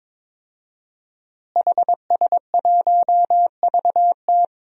Morse code, heard as a keyed tone, about 22 wpm, 700 Hz. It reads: HS1VT
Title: DXCC-100 Call Signs - 5 Characters - Repeated 3x - 22wpm